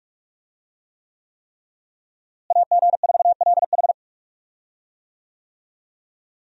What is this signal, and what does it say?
Morse code, heard as a keyed tone, about 43 wpm, 700 Hz. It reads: AG4LH
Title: USA Call Signs - 5 Characters - Repeated 3x - 45wpm